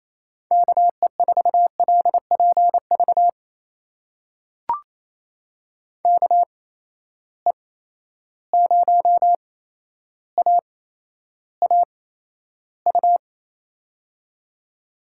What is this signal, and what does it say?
Morse code, heard as a keyed tone, about 28 wpm, spaced as for 12 wpm, 700 Hz. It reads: KE4LPV KE0AAU
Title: USA Call Signs - 6 Characters - 8x Character Spacing - 28wpm